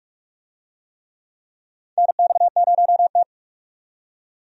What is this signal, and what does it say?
Morse code, heard as a keyed tone, about 46 wpm, 700 Hz. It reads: NX0T
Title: USA Call Signs - 4 Characters - 45wpm